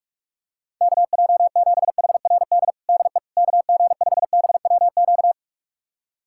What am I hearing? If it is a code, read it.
Morse code, heard as a keyed tone, about 46 wpm, 700 Hz. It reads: KJ7HRD DE KG5BWQ
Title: USA Callsign DE Callsign 45wpm